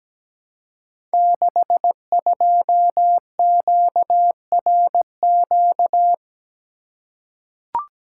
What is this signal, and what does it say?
Morse code, heard as a keyed tone, about 17 wpm, 700 Hz. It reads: 62QRQ
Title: Code Groups - 5 Letter:Numbers - 17wpm